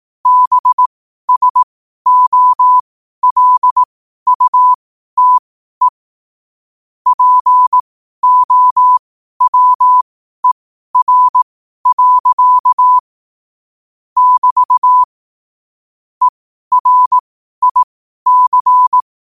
Morse code, heard as a keyed tone, about 18 wpm, spaced as for 12 wpm, 1000 Hz. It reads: BSOLUTE POWER. <BT> ERIC
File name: qotdfiles / 12wpm